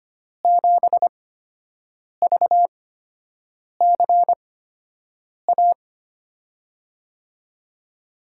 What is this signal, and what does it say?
Morse code, heard as a keyed tone, about 25 wpm, spaced as for 7 wpm, 700 Hz. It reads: 7VCA
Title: USA Call Signs - 8x Character Spacing - 25wpm